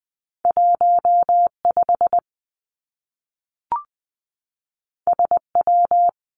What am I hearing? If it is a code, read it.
Morse code, heard as a keyed tone, about 20 wpm, 700 Hz. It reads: 1 5 S W
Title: Two Challenging Letter-Number Combinations 20wpm